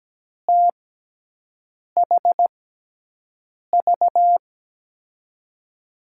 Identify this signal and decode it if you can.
Morse code, heard as a keyed tone, about 17 wpm, 700 Hz. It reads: T H V
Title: USA Call Signs - 6 Characters - 6x Character Spacing - 17wpm